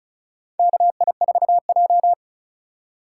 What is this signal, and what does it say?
Morse code, heard as a keyed tone, about 36 wpm, 700 Hz. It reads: KI4J
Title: USA Call Signs - 2x Character Spacing - 35wpm